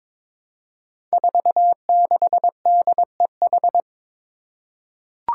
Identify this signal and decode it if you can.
Morse code, heard as a keyed tone, about 22 wpm, 700 Hz. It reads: 46DEH
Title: Five Challenging Letter-Number Combinations 22wpm